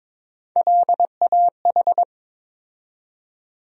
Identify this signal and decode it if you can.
Morse code, heard as a keyed tone, about 22 wpm, 700 Hz. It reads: LAH
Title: USA Call Sign Postfixes - 3 Letters 22wpm